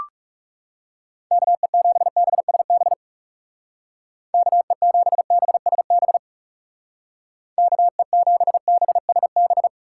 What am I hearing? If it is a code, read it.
Morse code, heard as a keyed tone, about 37 wpm, 700 Hz. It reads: KE7BSB KE7BSB KE7BSB
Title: Speed Racing - USA 2x3 Call Signs 30wpm